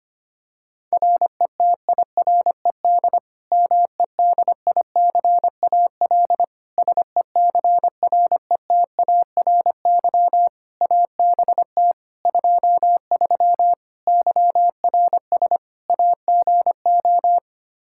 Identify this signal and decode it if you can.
Morse code, heard as a keyed tone, about 25 wpm, 700 Hz. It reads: RETIRED MEDICAL SECRETARY ABT 23 YRS AGO
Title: K1USN SST QSO Practice 25wpm